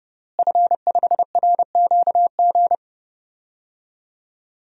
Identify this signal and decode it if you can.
Morse code, heard as a keyed tone, about 31 wpm, 700 Hz. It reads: F5RQG
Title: DXCC-100 Call Signs - 5 Characters - Repeated 3x - 30wpm